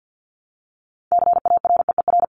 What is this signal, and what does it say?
Morse code, heard as a keyed tone, about 50 wpm, 700 Hz. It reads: CAREER
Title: Top 751-1000 Words 50wpm